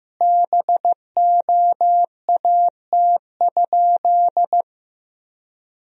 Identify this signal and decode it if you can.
Morse code, heard as a keyed tone, about 15 wpm, 700 Hz. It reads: BOAT?
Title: Sentences from Top 1000 Words - Review Part 3 - 15wpm